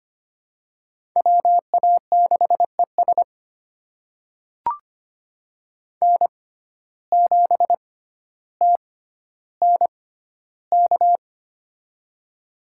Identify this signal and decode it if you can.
Morse code, heard as a keyed tone, about 25 wpm, spaced as for 13 wpm, 700 Hz. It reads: WA6ES N7TNK
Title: USA Call Signs - 6x Character Spacing - 25wpm